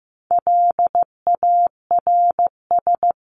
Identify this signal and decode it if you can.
Morse code, heard as a keyed tone, about 14 wpm, 700 Hz. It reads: LARS
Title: CWT Exchange 15wpm